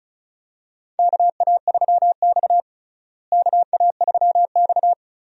Morse code, heard as a keyed tone, about 36 wpm, 700 Hz. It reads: KA3X KA3X
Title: USA Call Signs - 4 Characters - Repeated 3x - 35wpm